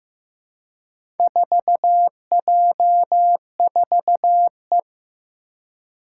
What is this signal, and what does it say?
Morse code, heard as a keyed tone, about 15 wpm, 700 Hz. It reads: 4J4E
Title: Four Challenging Letter-Number Combinations 15wpm